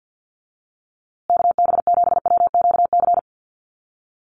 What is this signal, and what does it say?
Morse code, heard as a keyed tone, about 50 wpm, 700 Hz. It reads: KB7WQC
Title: USA Call Signs Repeated 2x 50wpm